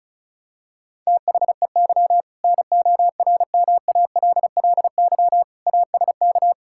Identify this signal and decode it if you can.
Morse code, heard as a keyed tone, about 36 wpm, 700 Hz. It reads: THEY NORMALLY ASK
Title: Sentences from Top 1000 Words - Review Part 6 - 35wpm